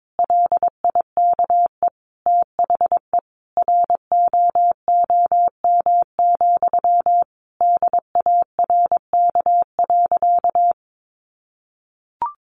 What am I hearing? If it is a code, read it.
Morse code, heard as a keyed tone, about 22 wpm, 700 Hz. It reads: LIKE THE ROOM, DARK.
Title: Sentences from Top 1400 Words 22wpm